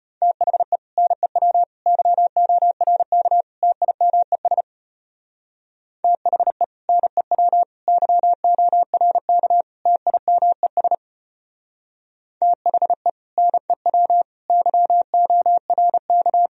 Morse code, heard as a keyed tone, about 30 wpm, 700 Hz. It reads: THE NEW YORK TIMES THE NEW YORK TIMES THE NEW YORK
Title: Speed-Racing - Sets of 4 Words 25wpm